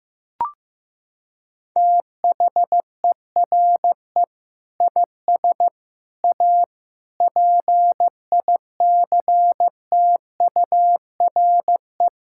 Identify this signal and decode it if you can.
Morse code, heard as a keyed tone, about 16 wpm, 700 Hz. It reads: THERE IS A PICTURE
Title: Sentences from Top 1000 Words - Review Part 6 - 15wpm